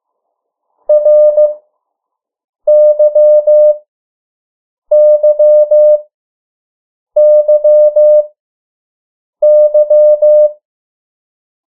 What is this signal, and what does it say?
Morse code, heard as a keyed tone, about 15 wpm, spaced as for 6 wpm, 600 Hz. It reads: RYYYY
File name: morse mp3 / 03 urok 2